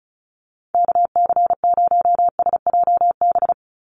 Morse code, heard as a keyed tone, about 34 wpm, 700 Hz. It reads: KC0SJB
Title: USA Call Signs 35wpm